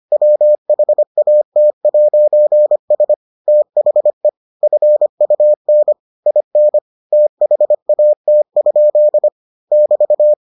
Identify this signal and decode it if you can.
Morse code, heard as a keyed tone, about 25 wpm, 600 Hz. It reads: WHAT'S THE FUN IN THAT? <BT>